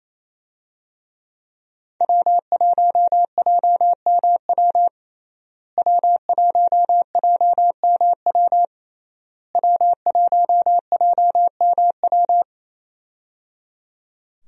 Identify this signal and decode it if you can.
Morse code, heard as a keyed tone, about 27 wpm, 700 Hz. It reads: W1JMW W1JMW W1JMW
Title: USA Call Signs - 5 Characters - Repeated 3x - 28wpm